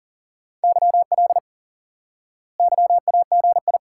Morse code, heard as a keyed tone, about 40 wpm, 700 Hz. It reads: YL YAGI
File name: IQE Course 3 - Lesson 89 - YAGI - Two Elements - 40wpm